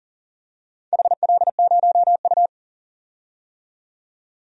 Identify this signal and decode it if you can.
Morse code, heard as a keyed tone, about 40 wpm, 700 Hz. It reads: HL0U
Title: DXCC-100 Call Signs - 4 Characters - 40wpm